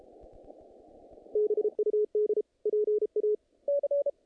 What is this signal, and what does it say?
Morse code, heard as a keyed tone, about 33 wpm, 420 Hz. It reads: BUD PA
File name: contests / naqp-cw-k5tr-op-k5pi- -2100